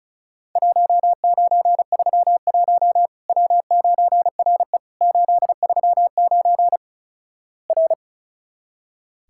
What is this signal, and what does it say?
Morse code, heard as a keyed tone, about 36 wpm, 700 Hz. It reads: 1931 W9RE 839 R
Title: Worked All Europe DX Contest - QTC Exchange - 35wpm